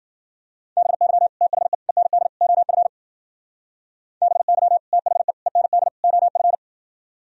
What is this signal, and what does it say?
Morse code, heard as a keyed tone, about 52 wpm, 700 Hz. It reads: BY THE END OF BY THE END OF